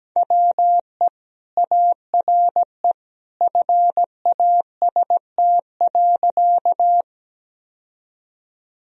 Morse code, heard as a keyed tone, about 17 wpm, 700 Hz. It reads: WE ARE FAST.
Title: Sentences from Top 1000 Words - Review Part 2 - 17wpm